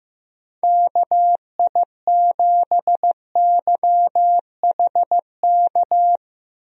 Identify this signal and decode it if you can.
Morse code, heard as a keyed tone, about 16 wpm, 700 Hz. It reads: KI7YHK
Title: USA Call Signs - 6 Characters - Repeated 3x - 15wpm